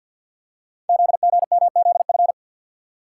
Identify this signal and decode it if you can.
Morse code, heard as a keyed tone, about 50 wpm, 690 Hz. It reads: ZGMZF
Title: Code Groups - 5 Letters - 50wpm